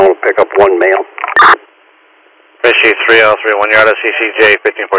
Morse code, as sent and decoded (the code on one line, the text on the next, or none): none